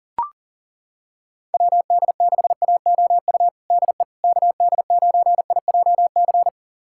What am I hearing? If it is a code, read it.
Morse code, heard as a keyed tone, about 37 wpm, 700 Hz. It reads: WD6AOU DE KD9IJC
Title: USA Callsign DE Callsign - 6 Characters - 40wpm